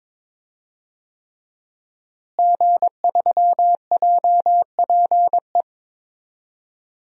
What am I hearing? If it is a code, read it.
Morse code, heard as a keyed tone, about 22 wpm, 700 Hz. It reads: G3JPE
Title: DXCC-100 Call Signs - 5 Characters - Repeated 3x - 22wpm